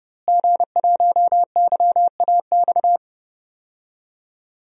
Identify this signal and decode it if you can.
Morse code, heard as a keyed tone, about 30 wpm, 700 Hz. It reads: G1YAX
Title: DXCC-100 Call Signs - 5 Characters - Repeated 3x - 30wpm